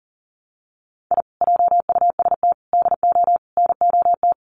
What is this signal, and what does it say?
Morse code, heard as a keyed tone, about 40 wpm, 700 Hz. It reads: I JUST DO NOT
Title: Sets of 4 Words 40wpm